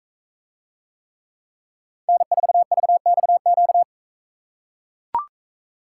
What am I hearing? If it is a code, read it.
Morse code, heard as a keyed tone, about 43 wpm, 700 Hz. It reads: N4VXQ